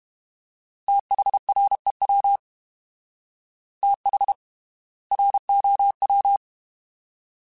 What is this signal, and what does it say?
Morse code, heard as a keyed tone, about 32 wpm, spaced as for 25 wpm, 800 Hz. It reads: THREW THROW
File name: TH / TH mp3s / Fast TH @30